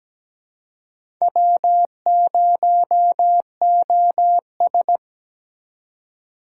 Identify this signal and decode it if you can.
Morse code, heard as a keyed tone, about 20 wpm, 700 Hz. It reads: W0OS
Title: Speed Racing - USA 1x2 Call Signs 17wpm